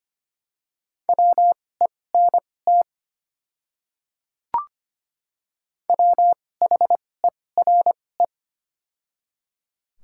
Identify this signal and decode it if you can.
Morse code, heard as a keyed tone, about 25 wpm, spaced as for 18 wpm, 700 Hz. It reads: WENT WHERE
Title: 2x Character Spacing - Top 300 Words - 25wpm